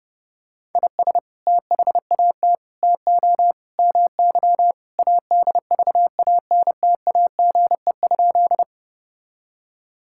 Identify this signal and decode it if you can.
Morse code, heard as a keyed tone, about 30 wpm, 700 Hz. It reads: IS THAT TO MY ADVANTAGE?
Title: Sentences from Top 1000 Words - Review Part 6 - 30wpm